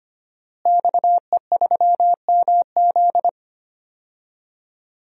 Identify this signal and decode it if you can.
Morse code, heard as a keyed tone, about 25 wpm, 700 Hz. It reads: XE3MZ
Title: DXCC-100 Call Signs - 5 Characters - Repeated 2x - 25wpm